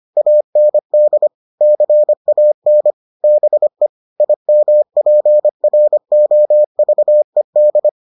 Morse code, heard as a keyed tone, about 25 wpm, 600 Hz. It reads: AND CAN BE IMPROVED